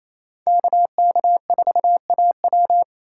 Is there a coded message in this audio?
Yes: Morse code from 0.5 to 2.8 s, heard as a keyed tone, about 36 wpm, 700 Hz. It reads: KK4AW